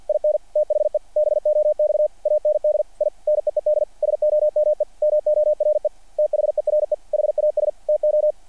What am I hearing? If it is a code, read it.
Morse code, heard as a keyed tone, about 48 wpm, 600 Hz. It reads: IN THE BOX AND I NEED SOME MORE THERE HAS TO
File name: archived / CRSnet